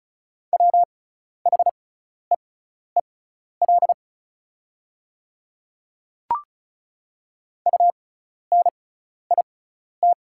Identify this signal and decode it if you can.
Morse code, heard as a keyed tone, about 34 wpm, spaced as for 12 wpm, 690 Hz. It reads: WHEEL UNIT
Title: 6x Character Spacing - Top 500 Words - 35wpm